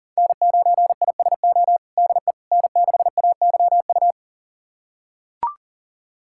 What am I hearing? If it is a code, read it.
Morse code, heard as a keyed tone, about 40 wpm, 700 Hz. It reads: N9ISO DE N6AYU
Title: USA Callsign DE Callsign - 5 Characters - 40wpm